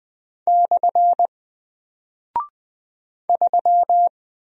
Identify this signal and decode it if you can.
Morse code, heard as a keyed tone, about 20 wpm, 700 Hz. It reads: / 3
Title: Single Character (Letter:Number:Punctuation:Prosign) Alt 20wpm